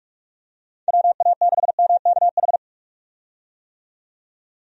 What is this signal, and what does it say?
Morse code, heard as a keyed tone, about 46 wpm, 700 Hz. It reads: WA6MKH
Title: USA Call Signs - 6 Characters - 45wpm